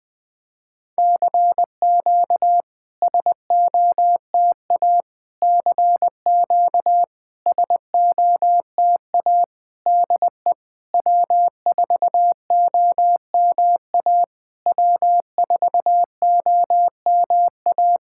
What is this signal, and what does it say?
Morse code, heard as a keyed tone, about 21 wpm, 700 Hz. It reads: CQ SOTA CQ SOTA DE W4OMA W4OMA
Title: SOTA QSO Practice 20wpm